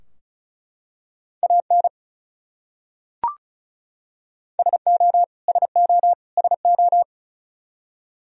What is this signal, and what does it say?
Morse code, heard as a keyed tone, about 34 wpm, 700 Hz. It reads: AN SO SO SO
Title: Top 100 Words Repeated 35wpm